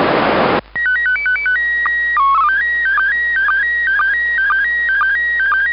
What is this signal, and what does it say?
SSTV (slow-scan television), PD 120.
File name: ik1yrc / AUDIO FOR QRZ / ISS 27 12 2020 0805 orbit2194